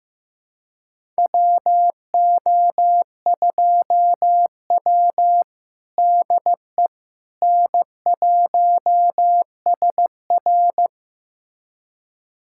Morse code, heard as a keyed tone, about 15 wpm, 700 Hz. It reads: WO2W DE N1SR
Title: USA Callsign DE Callsign - 4 Characters - 15wpm